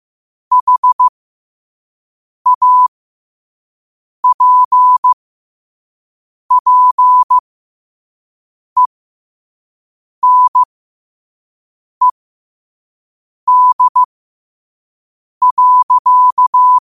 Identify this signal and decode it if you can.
Morse code, heard as a keyed tone, about 15 wpm, spaced as for 5 wpm, 1000 Hz. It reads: HAPPENED.
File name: qotdfiles / 5wpm